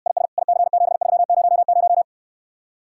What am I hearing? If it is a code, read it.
Morse code, heard as a keyed tone, about 69 wpm, 700 Hz. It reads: IS IC7300